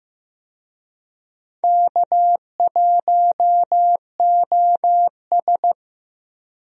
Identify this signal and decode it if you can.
Morse code, heard as a keyed tone, about 15 wpm, 700 Hz. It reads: K1OS
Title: USA Call Signs - 4 Characters - Repeated 3x - 15wpm